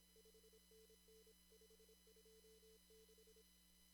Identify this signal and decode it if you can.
Morse code, heard as a keyed tone, about 25 wpm, 440 Hz, with noise. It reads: 5NN 526
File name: CW / kt5j wpx cw-op-k5pi